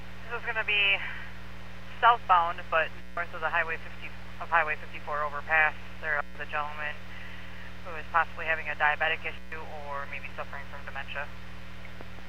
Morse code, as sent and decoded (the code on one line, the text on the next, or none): none